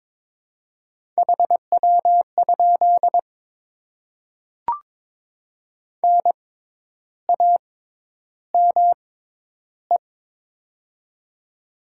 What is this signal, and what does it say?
Morse code, heard as a keyed tone, about 22 wpm, spaced as for 10 wpm, 700 Hz. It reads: HW? NAME
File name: Top 20 QSO Elements - 6x Character Spacing - 22wpm